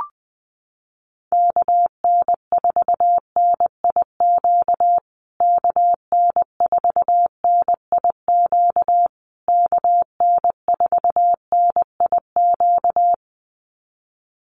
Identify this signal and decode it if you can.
Morse code, heard as a keyed tone, about 20 wpm, 700 Hz. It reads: KN4NIQ KN4NIQ KN4NIQ
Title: USA Call Signs Repeated 3x 20wpm